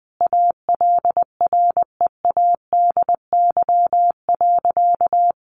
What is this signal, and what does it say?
Morse code, heard as a keyed tone, about 20 wpm, 700 Hz. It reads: ALREADY.
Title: Sentences from Top 1400 Words 20wpm